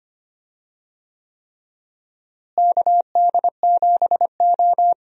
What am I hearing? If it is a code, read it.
Morse code, heard as a keyed tone, about 25 wpm, 700 Hz. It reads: KD7O
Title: USA Call Signs - 4 Characters - Repeated 3x - 25wpm